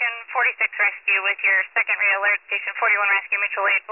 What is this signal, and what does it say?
Emergency Alert System (SAME) tones, heard as harsh buzzing bursts, with speech. It is heard from 0.5 to 3.2 s.